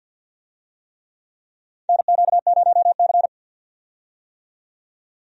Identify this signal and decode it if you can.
Morse code, heard as a keyed tone, about 51 wpm, 700 Hz. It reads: NQ0C
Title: USA Call Signs - 4 Characters - 50wpm